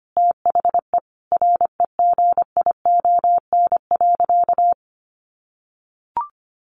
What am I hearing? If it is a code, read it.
Morse code, heard as a keyed tone, about 25 wpm, 700 Hz. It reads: THE REGION.
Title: Sentences from Top 1500 Words 25wpm